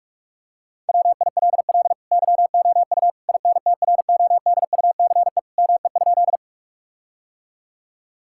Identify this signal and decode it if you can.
Morse code, heard as a keyed tone, about 46 wpm, 700 Hz. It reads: WILL YOU INTRODUCE ME?
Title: Sentences from Top 1600 Words 45wpm